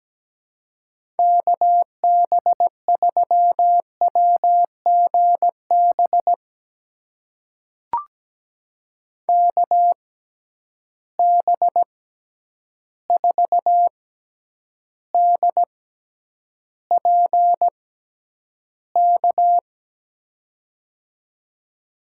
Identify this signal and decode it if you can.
Morse code, heard as a keyed tone, about 17 wpm, spaced as for 9 wpm, 700 Hz. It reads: KB3WGB KB4DPK
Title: USA Call Signs - 6 Characters - 6x Character Spacing - 17wpm